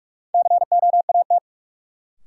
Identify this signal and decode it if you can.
Morse code, heard as a keyed tone, about 46 wpm, 700 Hz. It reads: COAT
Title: Sentences from Top 1600 Words - New Word Drill - 45wpm